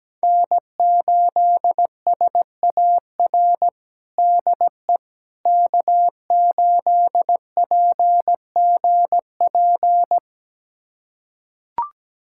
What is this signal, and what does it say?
Morse code, heard as a keyed tone, about 17 wpm, 700 Hz. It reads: N8SAR DE K8PGP
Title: USA Callsign DE Callsign - 5 Characters - 17wpm